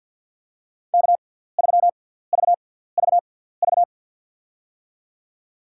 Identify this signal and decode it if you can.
Morse code, heard as a keyed tone, about 50 wpm, spaced as for 17 wpm, 690 Hz. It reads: K3VVV
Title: USA Call Signs - 5 Characters - 6x Character Spacing - 50wpm